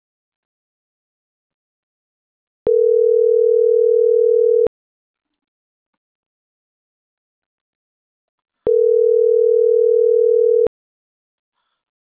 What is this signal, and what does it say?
Ringback tone (North America), 440 + 480 Hz.